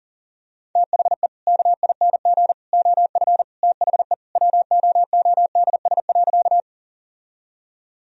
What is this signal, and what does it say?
Morse code, heard as a keyed tone, about 40 wpm, 700 Hz. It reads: THE KING OF THE WOODS.
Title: Sentences from Top 1000 Words - Review Part 2 - 40wpm